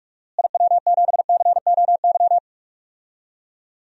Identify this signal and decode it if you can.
Morse code, heard as a keyed tone, about 46 wpm, 700 Hz. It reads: IW7COY
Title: DXCC-100 Call Signs - 6 Characters - Repeated 2x - 45wpm